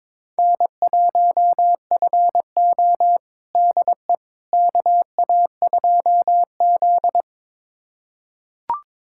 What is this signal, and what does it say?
Morse code, heard as a keyed tone, about 22 wpm, 700 Hz. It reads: N1FO DE KA2Z
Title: USA Callsign DE Callsign - 4 Characters - 22wpm